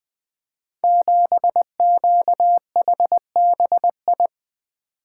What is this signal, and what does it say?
Morse code, heard as a keyed tone, about 20 wpm, 700 Hz. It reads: 7QHBI
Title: Code Groups - 5 Letter:Numbers - 20wpm